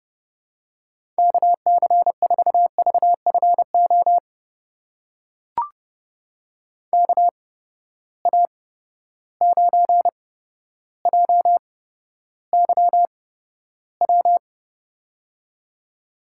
Morse code, heard as a keyed tone, about 30 wpm, spaced as for 13 wpm, 700 Hz. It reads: KC4VFO KA9JYW
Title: USA Call Signs - 6 Characters - 8x Character Spacing - 30wpm